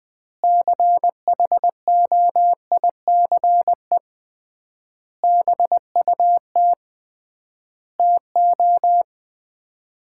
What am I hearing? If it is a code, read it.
Morse code, heard as a keyed tone, about 20 wpm, 690 Hz. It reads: CHOICE BUT TO
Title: 3x Word-Spacing - Sets of 5 Words - 20wpm